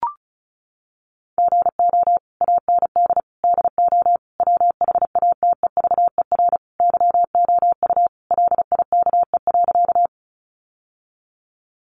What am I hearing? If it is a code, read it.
Morse code, heard as a keyed tone, about 34 wpm, 700 Hz. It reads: GO AND DO WHATEVER YOU LIKE.
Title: Sentences from Top 1100 Words 35wpm